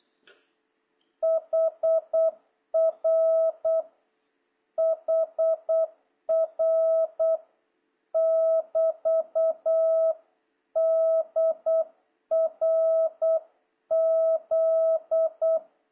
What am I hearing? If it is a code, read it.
Morse code, heard as a keyed tone, about 8 wpm, 650 Hz. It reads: HR HR <BT> DRZ
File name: jh8cbh / 2026cw